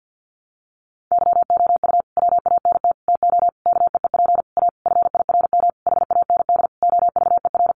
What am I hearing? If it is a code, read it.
Morse code, heard as a keyed tone, about 50 wpm, 700 Hz. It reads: YOU WANT TO KEEP A FIRM HAND OVER
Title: Sentences from Top 1400 Words 50wpm